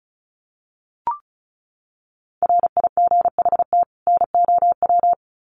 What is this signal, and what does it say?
Morse code, heard as a keyed tone, about 34 wpm, 700 Hz. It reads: RIGHT NOW
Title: Sets of 2 Words 35wpm